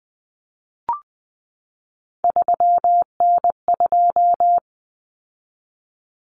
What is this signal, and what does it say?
Morse code, heard as a keyed tone, about 20 wpm, 700 Hz. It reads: 3N2
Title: Common Cut Numbers - 3-Digit 20wpm